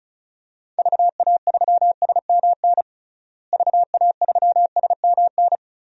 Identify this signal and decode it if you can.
Morse code, heard as a keyed tone, about 36 wpm, 700 Hz. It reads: VA3SMN VA3SMN
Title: DXCC-100 Call Signs - 6 Characters - Repeated 2x - 35wpm